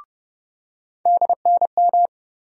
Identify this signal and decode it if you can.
Morse code, heard as a keyed tone, about 30 wpm, 700 Hz. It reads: DNM
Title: USA Call Sign Postfixes - 3 Letters 30wpm